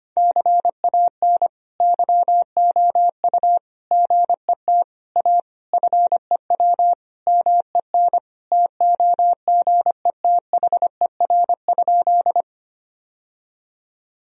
Morse code, heard as a keyed tone, about 25 wpm, 700 Hz. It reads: CAN YOU GET A FEW MEN TOGETHER?
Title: Sentences from Top 1000 Words - Review Part 1 - 25wpm